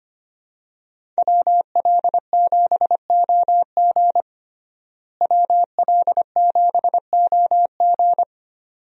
Morse code, heard as a keyed tone, about 25 wpm, 700 Hz. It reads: WL7OG WL7OG
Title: USA Call Signs - 5 Characters - Repeated 2x - 25wpm